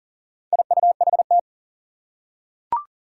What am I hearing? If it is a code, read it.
Morse code, heard as a keyed tone, about 37 wpm, 700 Hz. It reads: IUHT
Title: Four Challenging Letter-Number Combinations 40wpm